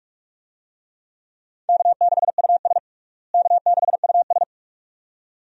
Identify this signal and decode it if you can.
Morse code, heard as a keyed tone, about 41 wpm, 700 Hz. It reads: K6US K6US
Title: USA Call Signs - 4 Characters - Repeated 2x - 45wpm